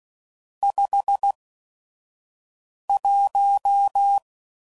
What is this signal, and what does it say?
Morse code, heard as a keyed tone, about 16 wpm, spaced as for 5 wpm, 800 Hz. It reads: 51
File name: CW Code Groups @5WPM / Weather Sets / Weather 4 21